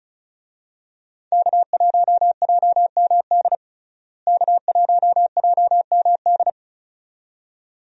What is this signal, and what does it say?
Morse code, heard as a keyed tone, about 36 wpm, 700 Hz. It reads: K1JMD K1JMD
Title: USA Call Signs - 5 Characters - Repeated 2x - 35wpm